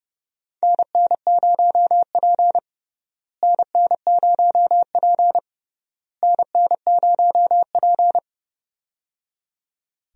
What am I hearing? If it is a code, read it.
Morse code, heard as a keyed tone, about 29 wpm, 700 Hz. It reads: NN0P NN0P NN0P
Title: USA Call Signs - 4 Characters - Repeated 3x - 30wpm